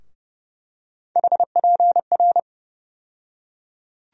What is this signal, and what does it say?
Morse code, heard as a keyed tone, about 32 wpm, 700 Hz. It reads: HPR
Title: USA Call Sign Postfixes - 3 Letters 30wpm